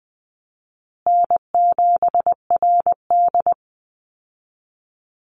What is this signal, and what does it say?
Morse code, heard as a keyed tone, about 20 wpm, 700 Hz. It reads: N7RD
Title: Easy Call Signs Repeated 2x - 20wpm